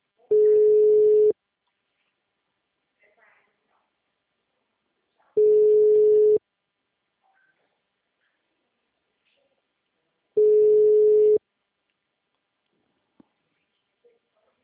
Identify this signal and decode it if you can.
Ringback tone (Europe, 425 Hz), with speech.